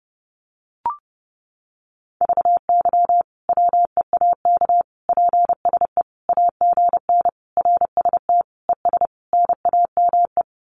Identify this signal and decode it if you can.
Morse code, heard as a keyed tone, about 30 wpm, 700 Hz. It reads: VY WEAK PSE AGN RST ES NAME
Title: Common QSO Phrases 30wpm